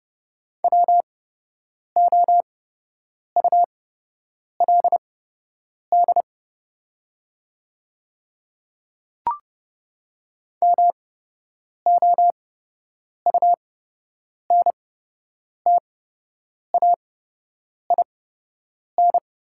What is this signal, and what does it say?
Morse code, heard as a keyed tone, about 30 wpm, spaced as for 8 wpm, 700 Hz. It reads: WOULD MOUNTAIN
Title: 8x Character Spacing - Top 500 Words - 30wpm